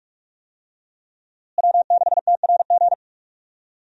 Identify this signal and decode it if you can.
Morse code, heard as a keyed tone, about 46 wpm, 700 Hz. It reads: W6TRG
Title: USA Call Signs - 5 Characters - Repeated 2x - 45wpm